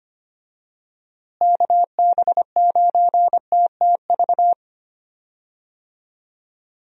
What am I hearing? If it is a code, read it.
Morse code, heard as a keyed tone, about 25 wpm, 700 Hz. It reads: KB9TTV
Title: USA Call Signs - 6 Characters - Repeated 3x - 25wpm